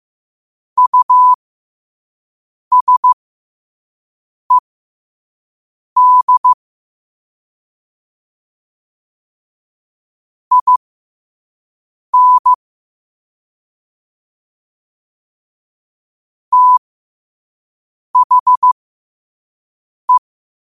Morse code, heard as a keyed tone, about 15 wpm, spaced as for 5 wpm, 1000 Hz. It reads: USED IN THE
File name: qotdfiles / 5wpm